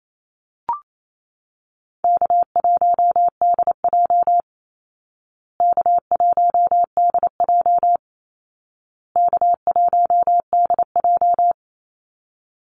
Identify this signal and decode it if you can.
Morse code, heard as a keyed tone, about 28 wpm, 700 Hz. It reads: K1DJ K1DJ K1DJ
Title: Easy Call Signs Repeated 3x - 28wpm